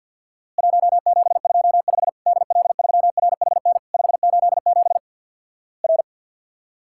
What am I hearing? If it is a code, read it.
Morse code, heard as a keyed tone, about 50 wpm, 700 Hz. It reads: 1725 DL3RHN 587 R
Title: Worked All Europe DX Contest - QTC Exchange with Repeats - 50wpm